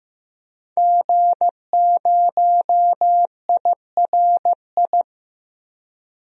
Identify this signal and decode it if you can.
Morse code, heard as a keyed tone, about 15 wpm, 700 Hz. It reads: G0IRI
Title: DXCC-100 Call Signs - 5 Characters - Repeated 3x - 15wpm